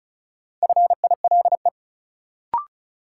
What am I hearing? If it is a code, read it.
Morse code, heard as a keyed tone, about 36 wpm, 700 Hz. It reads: FILE
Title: Sentences from Top 1200 Words - New Word Drill Alt - 35wpm